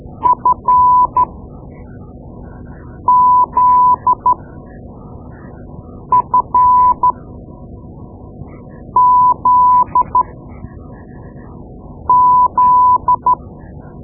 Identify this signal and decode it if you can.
Morse code, heard as a keyed tone, about 13 wpm, spaced as for 4 wpm, 1000 Hz. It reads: FZFZZ
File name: Morse Lessons / Morse Lessons / Tape-2B